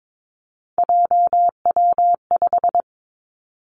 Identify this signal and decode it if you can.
Morse code, heard as a keyed tone, about 22 wpm, 700 Hz. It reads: JW5
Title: Three Challenging Letter-Number Combinations 22wpm